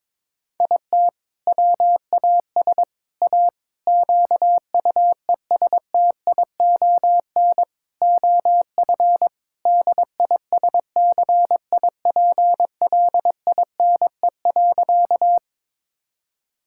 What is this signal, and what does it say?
Morse code, heard as a keyed tone, about 22 wpm, 700 Hz. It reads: IT WAS A QUESTION OF DISCIPLINE.
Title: Sentences from Top 1000 Words - Review Part 6 - 22wpm